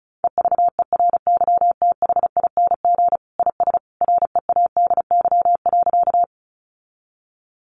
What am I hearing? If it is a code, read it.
Morse code, heard as a keyed tone, about 34 wpm, 700 Hz. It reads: EVERYTHING IS READY.
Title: Sentences from Top 700 Words 35wpm